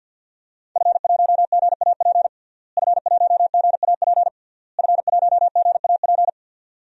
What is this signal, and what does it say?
Morse code, heard as a keyed tone, about 53 wpm, 700 Hz. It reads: F1GAP F1GAP F1GAP
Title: DXCC-100 Call Signs - 5 Characters - Repeated 3x - 50wpm